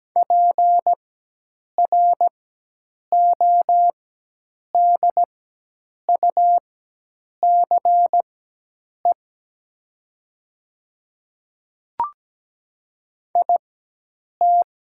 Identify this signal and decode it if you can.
Morse code, heard as a keyed tone, about 17 wpm, spaced as for 8 wpm, 700 Hz. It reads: PRODUCE IT
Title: 4x Character Spacing - Top 500 Words - 17wpm